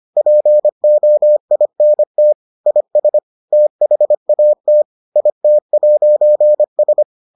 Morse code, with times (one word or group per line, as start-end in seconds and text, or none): 0.2-2.3 POINT
2.7-3.2 IS
3.5-4.8 THAT
5.2-7.0 IT'S